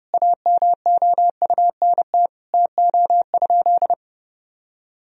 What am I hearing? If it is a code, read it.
Morse code, heard as a keyed tone, about 32 wpm, 700 Hz. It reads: AMOUNT TO?
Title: Sentences from Top 1000 Words - Review Part 4 - 30wpm